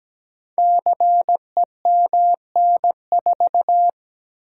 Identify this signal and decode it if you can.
Morse code, heard as a keyed tone, about 17 wpm, 700 Hz. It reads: CEMN4
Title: Code Groups - 5 Letter:Numbers - 17wpm